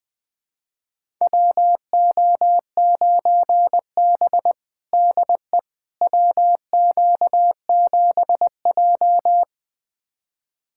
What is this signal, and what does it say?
Morse code, heard as a keyed tone, about 20 wpm, 700 Hz. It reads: WO9B DE WQ7J
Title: USA Callsign DE Callsign - 4 Characters - 20wpm